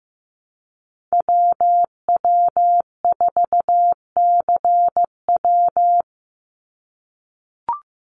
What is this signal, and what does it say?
Morse code, heard as a keyed tone, about 15 wpm, 700 Hz. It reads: WW4CW
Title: USA Call Signs 15wpm